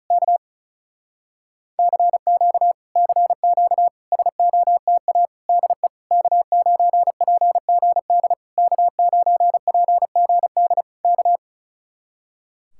Morse code, heard as a keyed tone, about 34 wpm, 700 Hz. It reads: K CQ CQ SOTA DE K9PGD K9PGD K
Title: Speed-Racing Common QSO Phrases Encore - 35wpm